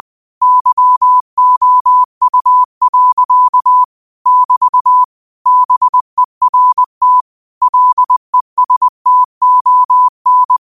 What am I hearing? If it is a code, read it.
Morse code, heard as a keyed tone, about 20 wpm, 1000 Hz. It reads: YOU. <BT> BERT LESTON
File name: qotdfiles / 20wpm